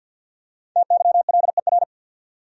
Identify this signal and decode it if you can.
Morse code, heard as a keyed tone, about 51 wpm, 700 Hz. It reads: TYLER
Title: USA Common Names - Max 5 Letters - 50wpm